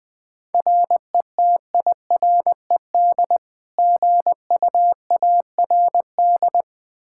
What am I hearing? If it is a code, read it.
Morse code, heard as a keyed tone, about 20 wpm, 700 Hz. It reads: RETIRED GUARD
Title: Practice QSO - Exchange 3 - More Info - 20wpm